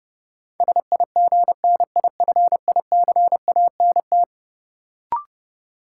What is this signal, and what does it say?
Morse code, heard as a keyed tone, about 30 wpm, 700 Hz. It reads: SIGNIFICANT
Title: Sentences from Top 1000 Words - Review Part 6 - New Word Drill - 30wpm